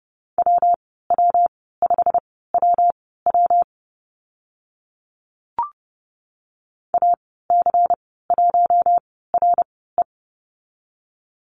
Morse code, heard as a keyed tone, about 30 wpm, spaced as for 17 wpm, 700 Hz. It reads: WW5WW AC1RE
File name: Lesson 19g - Letter w - Call Signs 30wpm